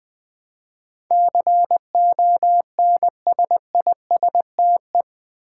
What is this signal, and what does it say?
Morse code, heard as a keyed tone, about 20 wpm, 700 Hz. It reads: CONSISTE
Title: Build A Word - 4 Letters At A Time - 12-Letter Words - 20wpm